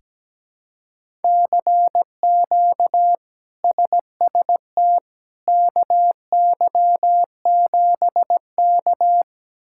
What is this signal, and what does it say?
Morse code, heard as a keyed tone, about 17 wpm, 700 Hz. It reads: CQ SST KY7K
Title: K1USN SST QSO Practice - 17wpm